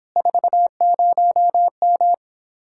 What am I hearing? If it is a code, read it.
Morse code, heard as a keyed tone, about 26 wpm, 700 Hz. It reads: 40M